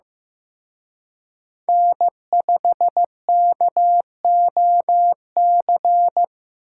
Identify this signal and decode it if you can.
Morse code, heard as a keyed tone, about 18 wpm, 700 Hz. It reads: N5KOC
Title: Speed Racing - USA 1x3 Call Signs 15wpm